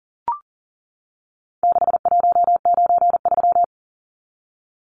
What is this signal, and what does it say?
Morse code, heard as a keyed tone, about 40 wpm, 700 Hz. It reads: 6193
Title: Numbers - 4-Digit 40wpm